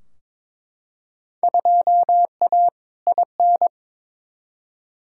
Morse code, heard as a keyed tone, about 23 wpm, 700 Hz. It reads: 2A IN
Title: Field Day Exchanges - 22wpm